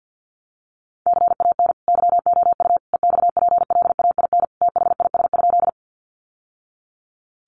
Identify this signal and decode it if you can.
Morse code, heard as a keyed tone, about 50 wpm, 700 Hz. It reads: CAN YOU EXPLAIN THIS?
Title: Sentences from Top 1100 Words 50wpm